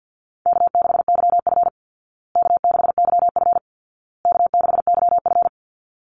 Morse code, heard as a keyed tone, about 48 wpm, 700 Hz. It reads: K6YF K6YF K6YF
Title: Easy Call Signs Repeated 3x - 50wpm